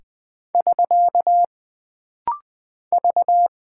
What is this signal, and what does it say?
Morse code, heard as a keyed tone, about 20 wpm, 700 Hz. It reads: <SK> V
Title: Single Character (Letter:Number:Punctuation:Prosign) - Rapid-Fire - 20wpm